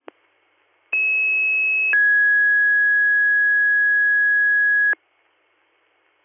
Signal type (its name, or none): two-tone page